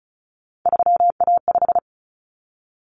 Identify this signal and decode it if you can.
Morse code, heard as a keyed tone, about 34 wpm, 700 Hz. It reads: 3A5
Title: Common Cut Numbers - 3-Digit 35wpm